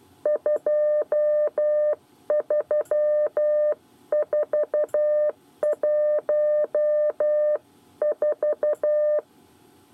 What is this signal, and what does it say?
Morse code, heard as a keyed tone, about 12 wpm, 600 Hz. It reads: 23414
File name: 8 bis lezione cw 1